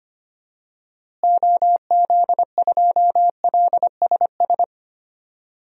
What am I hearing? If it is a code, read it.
Morse code, heard as a keyed tone, about 25 wpm, 700 Hz. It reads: OZ2LSS
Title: DXCC-100 Call Signs - 6 Characters - Repeated 3x - 25wpm